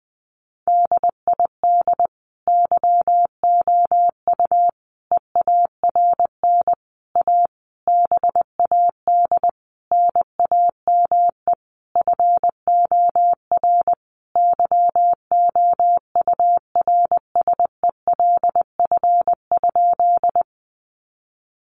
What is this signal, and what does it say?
Morse code, heard as a keyed tone, about 20 wpm, 700 Hz. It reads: DID YOU EARN A BAD NAME FOR YOURSELF?
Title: Sentences from Top 1500 Words 20wpm